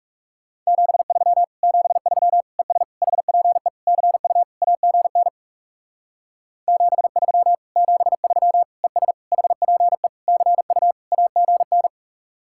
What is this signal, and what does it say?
Morse code, heard as a keyed tone, about 40 wpm, 700 Hz. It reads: 73 73 ES HPE CU AGN 73 73 ES HPE CU AGN